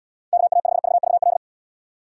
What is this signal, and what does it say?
Morse code, heard as a keyed tone, about 76 wpm, 700 Hz. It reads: KA5VVL